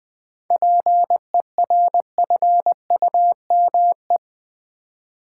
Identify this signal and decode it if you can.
Morse code, heard as a keyed tone, about 21 wpm, 700 Hz. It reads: PERFUME
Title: Easy Words with Hints - 20wpm